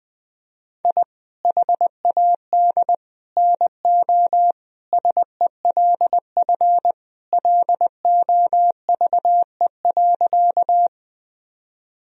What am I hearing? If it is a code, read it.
Morse code, heard as a keyed tone, about 20 wpm, 700 Hz. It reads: I HAD NO SELF LOVE.
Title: Sentences from Top 1000 Words - Review Part 1 - 20wpm